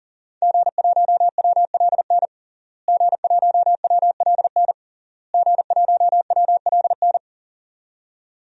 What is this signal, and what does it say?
Morse code, heard as a keyed tone, about 40 wpm, 700 Hz. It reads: G1WLN G1WLN G1WLN
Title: DXCC-100 Call Signs - 5 Characters - Repeated 3x - 40wpm